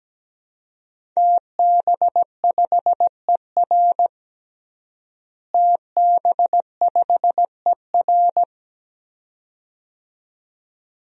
Morse code, heard as a keyed tone, about 17 wpm, 700 Hz. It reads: TB5ER TB5ER
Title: DXCC-100 Call Signs - 5 Characters - Repeated 2x - 17wpm